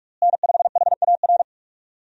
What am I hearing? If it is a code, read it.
Morse code, heard as a keyed tone, about 46 wpm, 700 Hz. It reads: N5HAR
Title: USA Call Signs - 5 Characters - Repeated 3x - 45wpm